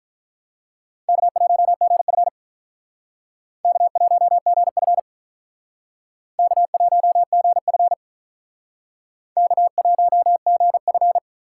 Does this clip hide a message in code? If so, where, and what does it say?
Yes: Morse code from 1.1 to 11.2 s, heard as a keyed tone, about 46 wpm, 700 Hz. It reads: K1GF K1GF K1GF K1GF